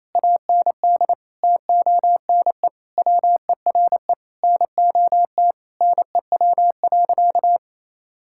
Morse code, heard as a keyed tone, about 28 wpm, 700 Hz. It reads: AND TONE WERE NOT NEW.
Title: Sentences from Top 1600 Words 28wpm